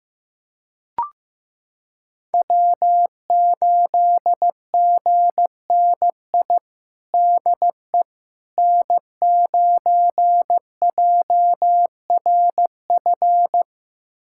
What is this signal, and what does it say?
Morse code, heard as a keyed tone, about 15 wpm, 700 Hz. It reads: W8GNI DE N9JRF
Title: USA Callsign DE Callsign - 5 Characters - 15wpm